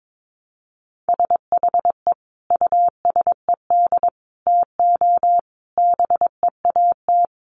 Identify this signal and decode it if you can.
Morse code, heard as a keyed tone, about 22 wpm, 700 Hz. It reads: SHE USED TO BEAT
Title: Sentences from Top 1300 Words 22wpm